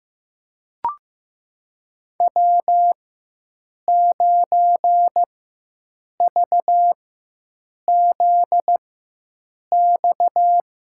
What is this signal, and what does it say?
Morse code, heard as a keyed tone, about 15 wpm, spaced as for 7 wpm, 700 Hz. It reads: W9VZX
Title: USA Call Signs - 4x Character Spacing - 15wpm